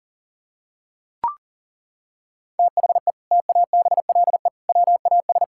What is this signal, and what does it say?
Morse code, heard as a keyed tone, about 40 wpm, 700 Hz. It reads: THE TABLE WAS
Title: Sentences from Top 1000 Words - Review Part 6 - 40wpm